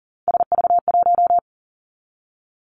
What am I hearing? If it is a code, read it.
Morse code, heard as a keyed tone, about 38 wpm, 700 Hz. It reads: SV1
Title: Callsign Prefixes for Top 50 DXCC - 40wpm